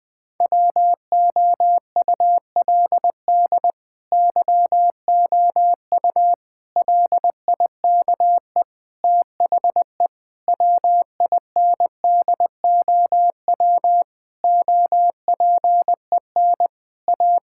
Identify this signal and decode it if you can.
Morse code, heard as a keyed tone, about 21 wpm, 700 Hz. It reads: WOULD YOU LIKE THE WINDOW OPEN A
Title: Sentences from Top 1600 Words 20wpm